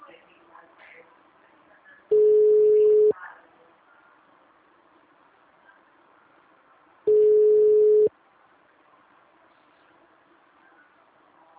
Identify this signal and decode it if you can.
Ringback tone (Europe, 425 Hz), with speech.
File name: out- -2001- - - .44513